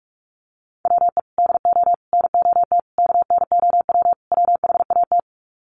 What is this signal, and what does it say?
Morse code, heard as a keyed tone, about 44 wpm, 700 Hz. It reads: WE DO NOT KNOW WHAT
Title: Sets of 5 Words 45wpm